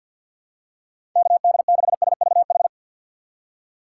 Morse code, heard as a keyed tone, about 51 wpm, 700 Hz. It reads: KD6SVH
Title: USA Call Signs - 4x Character Spacing - 50wpm